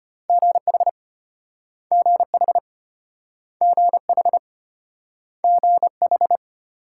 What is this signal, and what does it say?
Morse code, heard as a keyed tone, about 33 wpm, 700 Hz. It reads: GH GH GH GH